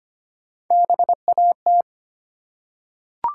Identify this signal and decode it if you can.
Morse code, heard as a keyed tone, about 25 wpm, 700 Hz. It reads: BAT
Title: Three Letter Words - 25wpm